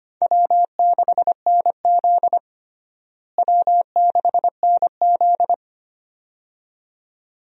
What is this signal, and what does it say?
Morse code, heard as a keyed tone, about 25 wpm, 700 Hz. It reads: W6NZ W6NZ
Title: USA Call Signs - 4 Characters - Repeated 2x - 25wpm